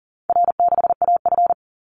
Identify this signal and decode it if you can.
Morse code, heard as a keyed tone, about 40 wpm, 700 Hz. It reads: R6AF
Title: Call Signs Repeated 2x - 40wpm